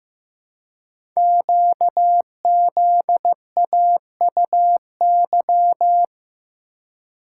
Morse code, heard as a keyed tone, about 15 wpm, 700 Hz. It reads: QZAUY
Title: Code Groups - 5 Letters - 15wpm